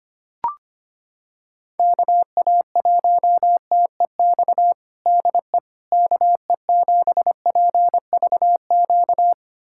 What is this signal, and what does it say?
Morse code, heard as a keyed tone, about 25 wpm, 700 Hz. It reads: KA1TEX DE KE7PVQ
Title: USA Callsign DE Callsign 25wpm